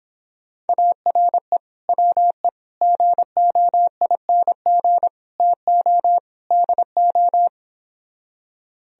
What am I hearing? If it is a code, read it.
Morse code, heard as a keyed tone, about 29 wpm, 700 Hz. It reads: ARE WE GOING TO DO